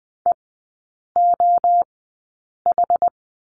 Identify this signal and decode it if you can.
Morse code, heard as a keyed tone, about 20 wpm, spaced as for 8 wpm, 700 Hz. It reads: EOH
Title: 060 Lesson 11e - Letter h - Three Characters 20wpm